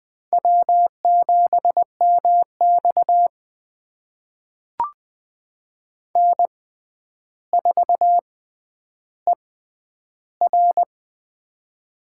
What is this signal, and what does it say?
Morse code, heard as a keyed tone, about 20 wpm, spaced as for 10 wpm, 700 Hz. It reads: W7MX N4ER
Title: USA Call Signs - 4 Characters - 6x Character Spacing - 20wpm